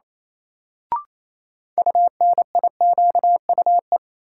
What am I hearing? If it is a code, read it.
Morse code, heard as a keyed tone, about 28 wpm, 700 Hz. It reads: UNIQUE
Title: Sentences from Top 1000 Words - Review Part 7 - New Word Drill - 28wpm